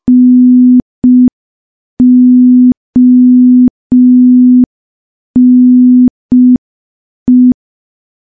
Morse code, heard as a keyed tone, about 5 wpm, 250 Hz. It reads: NONE
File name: morse / morse-training-05WPM